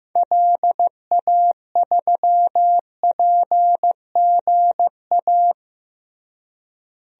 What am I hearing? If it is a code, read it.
Morse code, heard as a keyed tone, about 15 wpm, 700 Hz. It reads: LA3PGA
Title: DXCC-100 Call Signs - 6 Characters - Repeated 3x - 15wpm